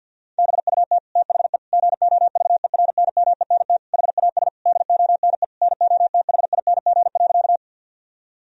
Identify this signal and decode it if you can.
Morse code, heard as a keyed tone, about 50 wpm, 700 Hz. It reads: BUT THE GOVERNMENT HAS DONE NOTHING.
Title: Sentences from Top 1000 Words - Review Part 3 - 50wpm